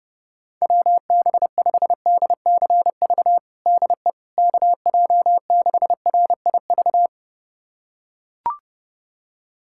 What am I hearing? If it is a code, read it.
Morse code, heard as a keyed tone, about 30 wpm, 700 Hz. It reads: WB5DCV DE KJ6RIV
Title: USA Callsign DE Callsign - 6 Characters - 30wpm